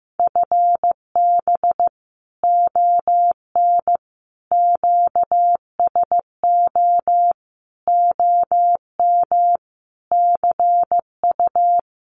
Morse code, heard as a keyed tone, about 15 wpm, 700 Hz. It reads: FB ON QSO OM CU
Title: Common QSO Phrases 15wpm